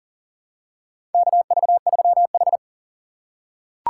Morse code, heard as a keyed tone, about 40 wpm, 700 Hz. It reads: KV3H
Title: USA Call Signs - 4 Characters - 4x Character Spacing - 40wpm